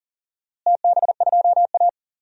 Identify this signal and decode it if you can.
Morse code, heard as a keyed tone, about 40 wpm, 700 Hz. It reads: TB2A
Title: DXCC-100 Call Signs - 4 Characters - Repeated 3x - 40wpm